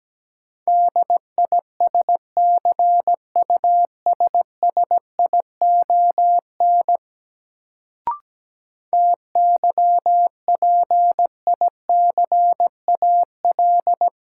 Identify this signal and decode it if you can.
Morse code, heard as a keyed tone, about 17 wpm, 700 Hz. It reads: DISCUSSION TYPICAL
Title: Sentences from Top 1100 Words - New Word Drill - 17wpm